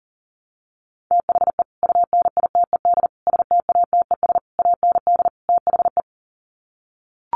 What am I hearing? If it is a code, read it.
Morse code, heard as a keyed tone, about 37 wpm, 700 Hz. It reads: THE UNITED STATES AND THE
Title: Sets of 5 Words 40wpm